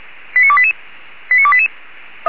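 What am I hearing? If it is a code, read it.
Five-tone selective call (ZVEI), heard as quick runs of five tones, tones of about 70 ms. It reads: 89288 89288
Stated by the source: Five-tone ZVEI selective call 89300, received by SDR.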